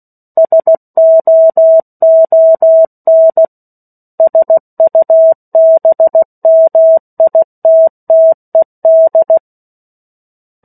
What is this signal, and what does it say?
Morse code, heard as a keyed tone, about 16 wpm, 650 Hz. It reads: SOON SUBMITTED
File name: 03-Chapter-03 16 WPM